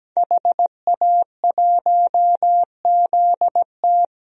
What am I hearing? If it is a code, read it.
Morse code, heard as a keyed tone, about 17 wpm, 700 Hz. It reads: HA1ZT
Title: DXCC-100 Call Signs - 5 Characters - Repeated 2x - 17wpm